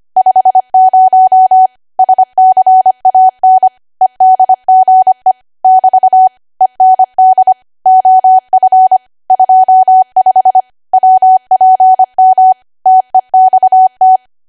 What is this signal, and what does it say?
Morse code, heard as a keyed tone, about 25 wpm, 750 Hz. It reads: 50 SCAN EDGE <BT> END OF 25 WPM TEXT
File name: Archive / 25 WPM / 25WPM